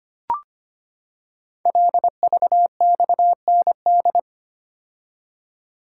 Morse code, heard as a keyed tone, about 25 wpm, 700 Hz. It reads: LVXND
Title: Code Groups - 5 Letters - 25wpm